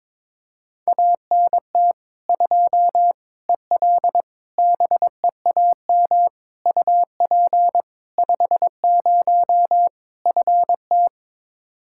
Morse code, heard as a keyed tone, about 22 wpm, 700 Hz. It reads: ANT 2 EL BEAM UP 50 FT
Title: K1USN SST QSO Practice 22wpm